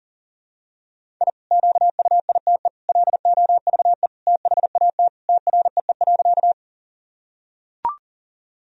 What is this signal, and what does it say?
Morse code, heard as a keyed tone, about 40 wpm, 700 Hz. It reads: I QUITE LOVE THAT TREE.
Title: Sentences from Top 1000 Words - Review Part 1 - 40wpm